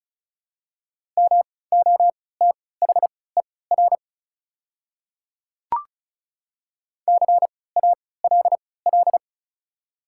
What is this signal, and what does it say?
Morse code, heard as a keyed tone, about 36 wpm, spaced as for 20 wpm, 700 Hz. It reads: MOTHER CALL
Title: 3x Character Spacing - Top 200 Words - 35wpm